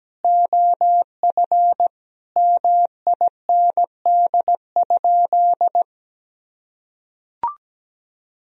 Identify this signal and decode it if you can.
Morse code, heard as a keyed tone, about 17 wpm, 700 Hz. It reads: OF MIND?
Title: Sentences from Top 1000 Words - Review Part 2 - 17wpm